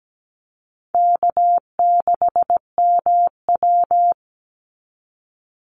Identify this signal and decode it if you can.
Morse code, heard as a keyed tone, about 17 wpm, 700 Hz. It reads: K6MW
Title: NA SOTA Chaser Callsigns 17wpm